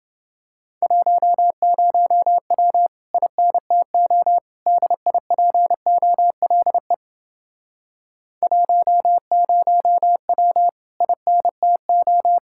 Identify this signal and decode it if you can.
Morse code, heard as a keyed tone, about 29 wpm, 700 Hz. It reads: 10W INTO DIPOLE 10W INTO